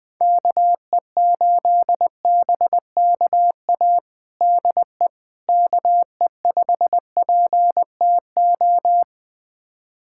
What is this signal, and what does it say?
Morse code, heard as a keyed tone, about 21 wpm, 700 Hz. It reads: KE8BKA DE KE5PTO
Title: USA Callsign DE Callsign - 6 Characters - 20wpm